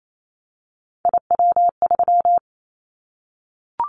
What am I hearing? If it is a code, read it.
Morse code, heard as a keyed tone, about 28 wpm, 700 Hz. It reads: IW3
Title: Callsign Prefixes for Top 25 DXCC - 28wpm